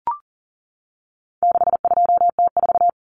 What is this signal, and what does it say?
Morse code, heard as a keyed tone, about 40 wpm, 700 Hz. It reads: 62T4
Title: Common Cut Numbers - 4-Digit 40wpm